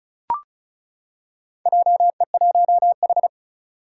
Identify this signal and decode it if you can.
Morse code, heard as a keyed tone, about 36 wpm, 700 Hz. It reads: JE1H
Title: Four Challenging Letter-Number Combinations 35wpm